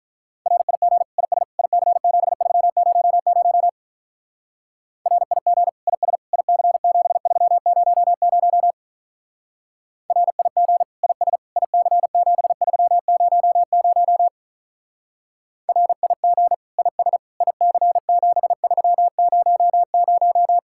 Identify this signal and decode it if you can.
Morse code, heard as a keyed tone, about 42 wpm, 700 Hz. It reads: RIG IS IC7300 RIG IS IC7300 RIG IS IC7300 RIG IS IC7300
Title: Speed-Racing Common QSO Phrases Encore - 35wpm